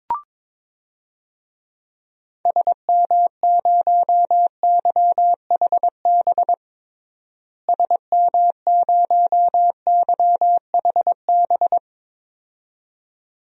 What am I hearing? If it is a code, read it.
Morse code, heard as a keyed tone, about 22 wpm, 700 Hz. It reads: SM0YHB SM0YHB
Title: DXCC-100 Call Signs - 6 Characters - Repeated 2x - 22wpm